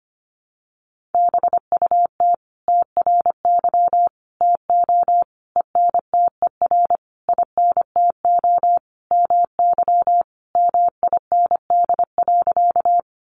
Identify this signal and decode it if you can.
Morse code, heard as a keyed tone, about 25 wpm, 700 Hz. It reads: BUT TRY TO ENTER INTO MY MIND.
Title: Sentences from Top 1300 Words 25wpm